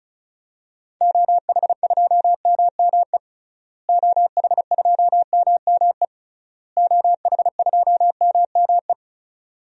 Morse code, heard as a keyed tone, about 34 wpm, 700 Hz. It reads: OH2MME OH2MME OH2MME
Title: DXCC-100 Call Signs - 6 Characters - Repeated 3x - 35wpm